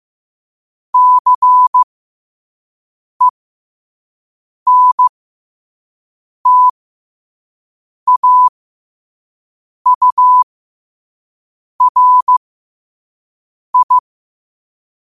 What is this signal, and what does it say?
Morse code, heard as a keyed tone, about 15 wpm, spaced as for 5 wpm, 1000 Hz. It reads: CENTAURI
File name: qotdfiles / 5wpm